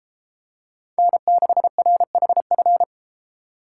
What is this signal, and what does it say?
Morse code, heard as a keyed tone, about 33 wpm, 700 Hz. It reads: N6RHF